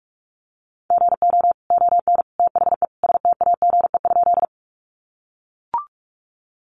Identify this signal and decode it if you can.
Morse code, heard as a keyed tone, about 44 wpm, 700 Hz. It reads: GO ON THE STAGE?
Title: Sentences from Top 1200 Words 45wpm